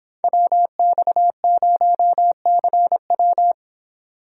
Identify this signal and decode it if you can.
Morse code, heard as a keyed tone, about 29 wpm, 700 Hz. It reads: WX0CW